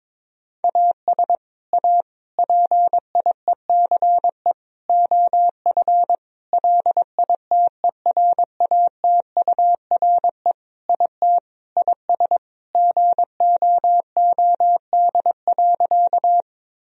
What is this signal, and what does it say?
Morse code, heard as a keyed tone, about 22 wpm, 700 Hz. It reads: AS A PIECE OF LITERATURE IT IS GOOD.
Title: Sentences from Top 1000 Words - Review Part 3 - 22wpm